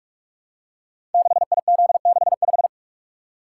Morse code, heard as a keyed tone, about 44 wpm, 700 Hz. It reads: 6IZ65
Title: Code Groups - 5 Letter:Numbers - 45wpm